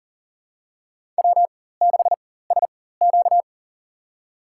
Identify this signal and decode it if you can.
Morse code, heard as a keyed tone, about 40 wpm, spaced as for 19 wpm, 700 Hz. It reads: W6SQ
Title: USA Call Signs - 4 Characters - 4x Character Spacing - 40wpm